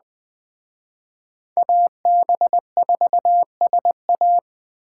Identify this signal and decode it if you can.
Morse code, heard as a keyed tone, about 20 wpm, 700 Hz. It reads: AB4SA